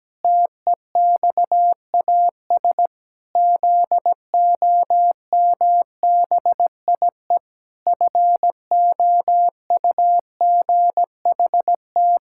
Morse code, heard as a keyed tone, about 17 wpm, 700 Hz. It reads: TEXAS ZOMBIE FOUGHT
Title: Pangram Challenge v2 - 17wpm